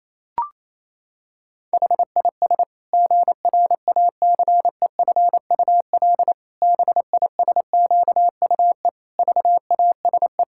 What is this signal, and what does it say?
Morse code, heard as a keyed tone, about 28 wpm, 700 Hz. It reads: HIS GRACEFUL BISQUE VASE
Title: Pangram Challenge v2 - 28wpm